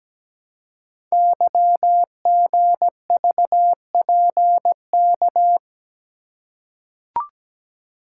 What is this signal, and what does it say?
Morse code, heard as a keyed tone, about 17 wpm, 700 Hz. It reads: YGVPK
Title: Code Groups - 5 Letters - 17wpm